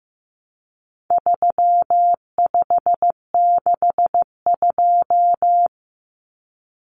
Morse code, heard as a keyed tone, about 15 wpm, 700 Hz. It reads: 3562
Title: Numbers - 4-Digit 15wpm